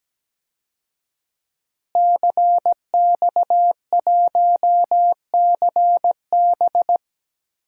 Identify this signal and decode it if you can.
Morse code, heard as a keyed tone, about 17 wpm, 700 Hz. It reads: CX1CB
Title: DXCC-100 Call Signs - 5 Characters - Repeated 3x - 17wpm